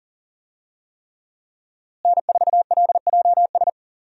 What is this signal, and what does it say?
Morse code, heard as a keyed tone, about 40 wpm, 700 Hz. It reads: N4LJS
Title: USA Call Signs - 5 Characters - Repeated 3x - 40wpm